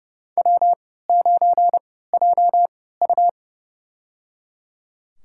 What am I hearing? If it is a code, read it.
Morse code, heard as a keyed tone, about 30 wpm, spaced as for 17 wpm, 700 Hz. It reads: W9JU
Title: USA Call Signs - 4 Characters - 3x Character Spacing - 30wpm